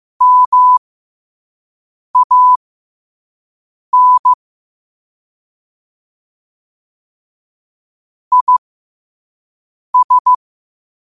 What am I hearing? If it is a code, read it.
Morse code, heard as a keyed tone, about 15 wpm, spaced as for 5 wpm, 1000 Hz. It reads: MAN IS
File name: qotdfiles / 5wpm